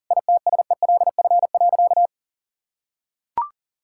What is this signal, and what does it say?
Morse code, heard as a keyed tone, about 40 wpm, 700 Hz. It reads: ITSELF.
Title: Sentences from Top 1000 Words - Review Part 5 - 40wpm